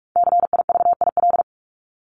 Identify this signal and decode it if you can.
Morse code, heard as a keyed tone, about 44 wpm, 700 Hz. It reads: CIVIL
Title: Top 1751-2000 Words 45wpm